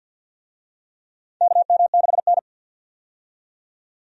Morse code, heard as a keyed tone, about 50 wpm, 700 Hz. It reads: KM6N
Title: USA Call Signs - 4 Characters - 50wpm